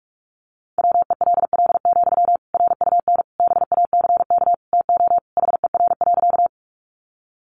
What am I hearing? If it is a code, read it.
Morse code, heard as a keyed tone, about 44 wpm, 700 Hz. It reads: WELL, RUN BACK TO HER.
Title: Sentences from Top 300 Words 45wpm